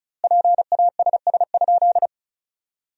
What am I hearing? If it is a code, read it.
Morse code, heard as a keyed tone, about 34 wpm, 700 Hz. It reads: PASS?
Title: 2x Word-Spacing - Sentences from Top 400 Words - 35wpm